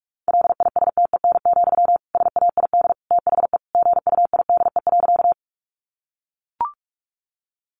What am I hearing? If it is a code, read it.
Morse code, heard as a keyed tone, about 44 wpm, 700 Hz. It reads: LISTEN, SAID THE GUIDE.
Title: Sentences from Top 600 Words 45wpm